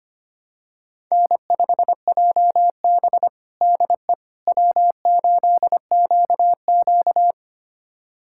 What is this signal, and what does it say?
Morse code, heard as a keyed tone, about 25 wpm, 700 Hz. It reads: N5JB DE W8QQ
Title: USA Callsign DE Callsign - 4 Characters - 25wpm